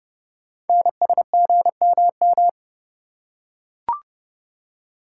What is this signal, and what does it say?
Morse code, heard as a keyed tone, about 30 wpm, 700 Hz. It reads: NSGMM
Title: Code Groups - 5 Letters - 30wpm